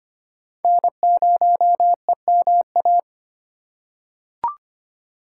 Morse code, heard as a keyed tone, about 25 wpm, 700 Hz. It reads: N0EMA
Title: USA Call Signs - 5 Characters - 2x Character Spacing - 25wpm